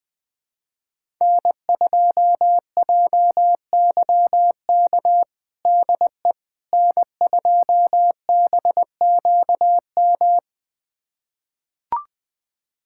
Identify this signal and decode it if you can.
Morse code, heard as a keyed tone, about 21 wpm, 700 Hz. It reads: N2JYK DE N2BQM
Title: USA Callsign DE Callsign - 5 Characters - 20wpm